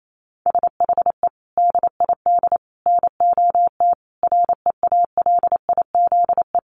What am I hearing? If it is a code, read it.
Morse code, heard as a keyed tone, about 28 wpm, 700 Hz. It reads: SHE DID NOT REALIZE
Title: Sentences from Top 1000 Words 28wpm